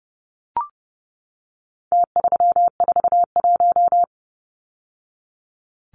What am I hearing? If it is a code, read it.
Morse code, heard as a keyed tone, about 30 wpm, 700 Hz. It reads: T341
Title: Numbers - Contest Style - 4-Digit 30wpm